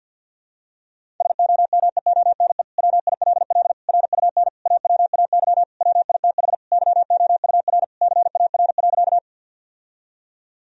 Morse code, heard as a keyed tone, about 48 wpm, 700 Hz. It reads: SOMEONE WILL RUN AWAY WITH YOUR CAR.
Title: Sentences from Top 1000 Words - Review Part 1 - 50wpm